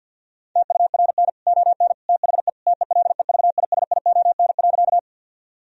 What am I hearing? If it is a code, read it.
Morse code, heard as a keyed tone, about 50 wpm, 700 Hz. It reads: TURN ON THE TELEVISION.
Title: Sentences from Top 1000 Words - Review Part 4 - 50wpm